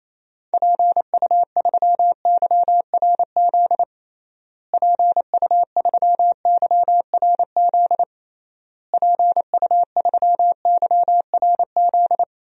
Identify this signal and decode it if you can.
Morse code, heard as a keyed tone, about 28 wpm, 700 Hz. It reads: PU3YRZ PU3YRZ PU3YRZ
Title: DXCC-100 Call Signs - 6 Characters - Repeated 3x - 28wpm